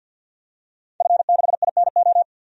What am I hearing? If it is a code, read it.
Morse code, heard as a keyed tone, about 47 wpm, 700 Hz. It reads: F6INO
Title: DXCC-100 Call Signs - 5 Characters - Repeated 2x - 50wpm